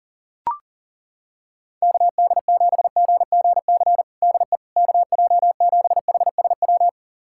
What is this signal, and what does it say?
Morse code, heard as a keyed tone, about 40 wpm, 700 Hz. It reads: KD7GGC DE KJ7HSW
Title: USA Callsign DE Callsign - 6 Characters - 40wpm